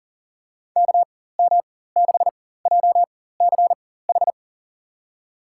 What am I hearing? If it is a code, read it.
Morse code, heard as a keyed tone, about 40 wpm, spaced as for 19 wpm, 700 Hz. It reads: KM6JCH
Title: USA Call Signs - 6 Characters - 4x Character Spacing - 40wpm